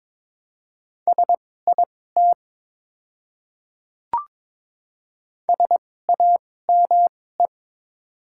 Morse code, heard as a keyed tone, about 22 wpm, spaced as for 16 wpm, 690 Hz. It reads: SIT SAME
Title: 2x Character Spacing - Top 500 Words - 22wpm